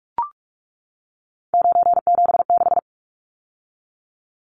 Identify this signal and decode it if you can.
Morse code, heard as a keyed tone, about 44 wpm, 700 Hz. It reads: 976
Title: Numbers - 3-Digit 45wpm